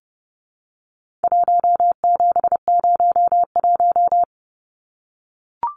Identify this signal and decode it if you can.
Morse code, heard as a keyed tone, about 30 wpm, 700 Hz. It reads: 1701
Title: Numbers - 4-Digit 30wpm